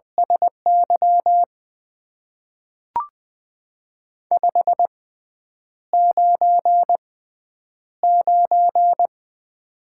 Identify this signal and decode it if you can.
Morse code, heard as a keyed tone, about 20 wpm, spaced as for 9 wpm, 700 Hz. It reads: SY 599
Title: Top 90 QSO Elements - 6x Character Spacing - 20wpm